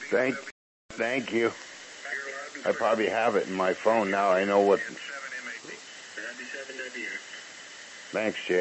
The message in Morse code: none

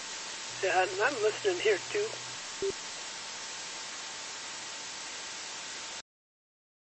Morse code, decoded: none

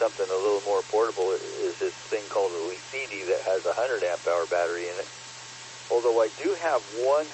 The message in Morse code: none

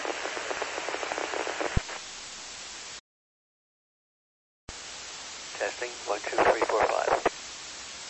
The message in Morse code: none

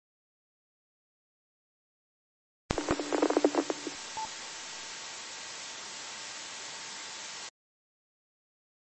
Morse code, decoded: none

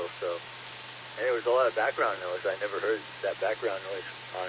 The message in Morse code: none

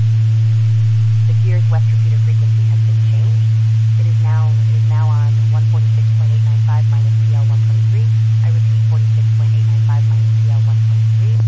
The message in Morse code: none

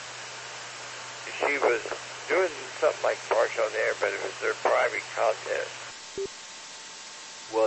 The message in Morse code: none

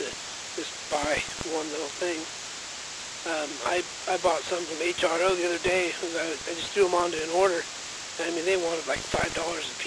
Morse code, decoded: none

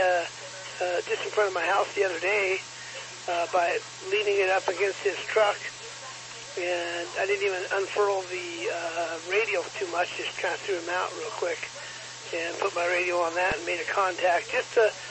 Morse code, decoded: none

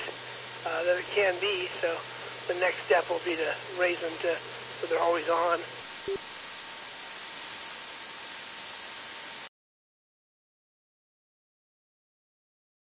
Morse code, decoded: none

